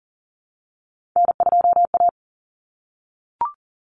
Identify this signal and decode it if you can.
Morse code, heard as a keyed tone, about 40 wpm, 700 Hz. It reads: N2A
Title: Common Cut Numbers - 3-Digit 40wpm